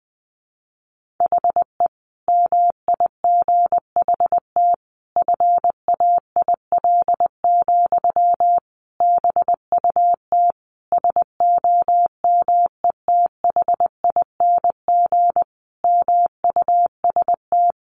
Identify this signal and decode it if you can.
Morse code, heard as a keyed tone, about 20 wpm, 700 Hz. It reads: HE MIGHT FAIL, BUT SOMETHING MUST
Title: Sentences from Top 1100 Words 20wpm